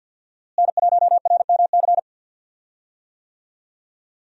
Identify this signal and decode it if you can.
Morse code, heard as a keyed tone, about 52 wpm, 700 Hz. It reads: N1RMC
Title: USA Call Signs - 5 Characters - Repeated 3x - 50wpm